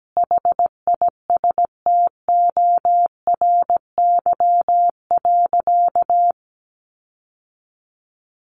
Morse code, heard as a keyed tone, about 17 wpm, 700 Hz. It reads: HISTORY.
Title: Sentences from Top 1300 Words 17wpm